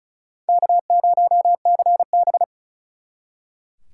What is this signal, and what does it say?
Morse code, heard as a keyed tone, about 47 wpm, 700 Hz. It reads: K0CB